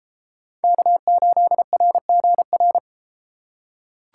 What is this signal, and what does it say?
Morse code, heard as a keyed tone, about 31 wpm, 700 Hz. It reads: K8RGR